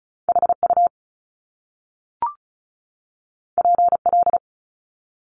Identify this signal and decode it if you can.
Morse code, heard as a keyed tone, about 36 wpm, 700 Hz. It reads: HU PL
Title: Top Two Letter Combinations in Words 35wpm